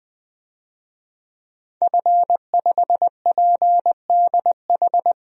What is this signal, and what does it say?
Morse code, heard as a keyed tone, about 20 wpm, 700 Hz. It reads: F5PDH
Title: DXCC-100 Call Signs - 5 Characters - 20wpm